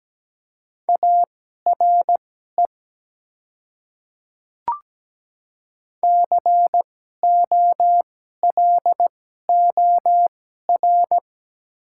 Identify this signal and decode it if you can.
Morse code, heard as a keyed tone, about 17 wpm, spaced as for 12 wpm, 700 Hz. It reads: ARE COLOR
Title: 2x Character Spacing - Top 300 Words - 17wpm